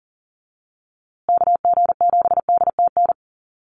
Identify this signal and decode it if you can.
Morse code, heard as a keyed tone, about 40 wpm, 700 Hz. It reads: KG7DTN
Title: USA Call Signs Repeated 2x 40wpm